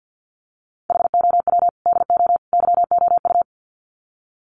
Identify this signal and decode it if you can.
Morse code, heard as a keyed tone, about 50 wpm, 700 Hz. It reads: HOW DO YOU
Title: Sets of 3 Words 50wpm